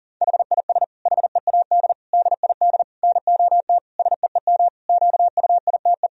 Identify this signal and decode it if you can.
Morse code, heard as a keyed tone, about 40 wpm, 700 Hz. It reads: HIS HEAD DID NOT SEEM QUITE
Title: Sentences from Top 1000 Words - Review Part 2 - 40wpm